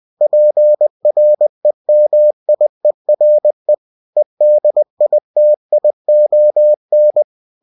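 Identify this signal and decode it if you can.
Morse code, heard as a keyed tone, about 20 wpm, 600 Hz. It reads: PREMIERE EDITION
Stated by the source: Bulletin sent in CW.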